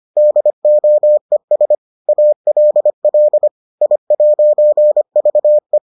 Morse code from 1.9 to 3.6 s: ALL

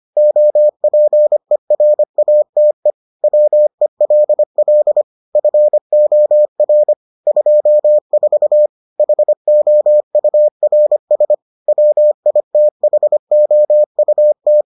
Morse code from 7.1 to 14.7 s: 24 HOURS WITHOUT

